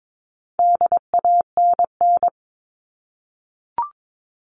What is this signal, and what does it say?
Morse code, heard as a keyed tone, about 22 wpm, 700 Hz. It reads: DANN
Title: All Cut Numbers - 4-Digit 22wpm